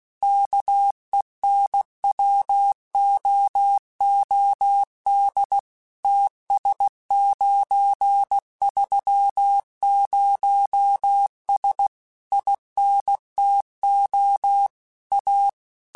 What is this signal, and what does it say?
Morse code, heard as a keyed tone, about 16 wpm, 800 Hz. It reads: KENWOOD TS930S INTO A
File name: QSOs/QSOs-mp3s/kc7haa@15